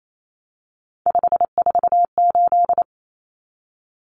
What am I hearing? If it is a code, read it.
Morse code, heard as a keyed tone, about 28 wpm, 700 Hz. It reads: 548
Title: Numbers - 3-Digit 28wpm